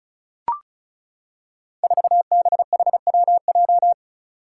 Morse code, heard as a keyed tone, about 36 wpm, 700 Hz. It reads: 4BHWJ
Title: Five Challenging Letter-Number Combinations 35wpm